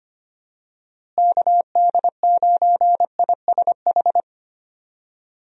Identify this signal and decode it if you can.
Morse code, heard as a keyed tone, about 25 wpm, 700 Hz. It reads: KD9ISH
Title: USA Call Signs - 6 Characters - Repeated 3x - 25wpm